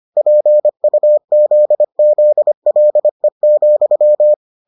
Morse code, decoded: PUZZLE,

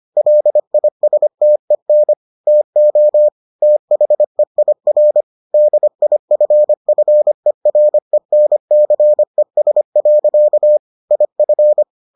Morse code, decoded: LISTEN TO THEIR DIFFERENCES. IF

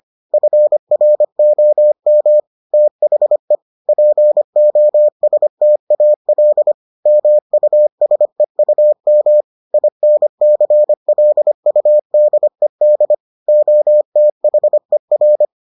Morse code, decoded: FROM THE POSTAL MUSEUM INCLUDED OTHER

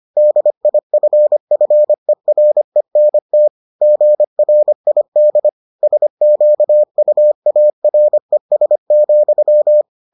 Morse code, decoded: DIFFERENT GRID SQUARES,

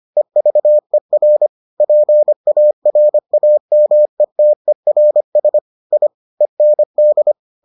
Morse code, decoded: EVER PARAMETERS I END

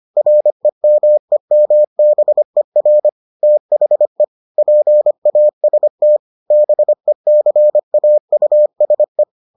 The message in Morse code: REMEMBER THE PAST BECAUSE

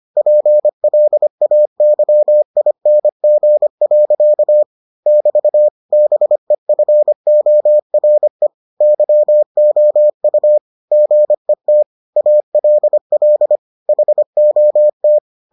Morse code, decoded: PLAYING. <BT> BEFORE YOU GET ALL HOT

